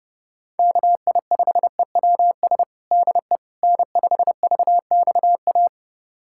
Morse code, heard as a keyed tone, about 32 wpm, 700 Hz. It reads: KI5EWS DE N5VXA
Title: USA Callsign DE Callsign 30wpm